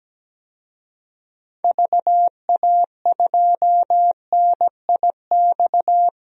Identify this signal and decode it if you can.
Morse code, heard as a keyed tone, about 17 wpm, 700 Hz. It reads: VA2NIX
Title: DXCC-100 Call Signs - 6 Characters - 17wpm